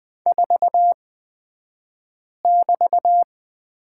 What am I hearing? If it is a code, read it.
Morse code, heard as a keyed tone, about 20 wpm, spaced as for 5 wpm, 700 Hz. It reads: 4<BT>
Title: 223 Lesson 36d - BT - Two Characters 20wpm